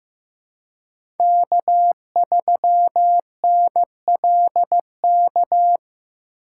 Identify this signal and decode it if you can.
Morse code, heard as a keyed tone, about 15 wpm, 700 Hz. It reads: K3NLK
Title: USA Call Signs - 5 Characters - 2x Character Spacing - 15wpm